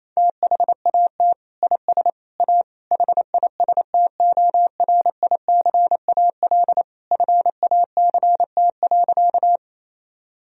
Morse code, heard as a keyed tone, about 27 wpm, 700 Hz. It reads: THAT IS A HISTORICAL FACT.
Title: Sentences from Top 1000 Words - Review Part 4 - 28wpm